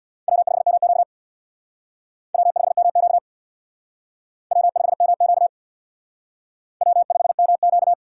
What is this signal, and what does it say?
Morse code, heard as a keyed tone, about 61 wpm, 700 Hz. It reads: W5MQ W5MQ W5MQ W5MQ
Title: Speed Racing - USA Call Signs 50wpm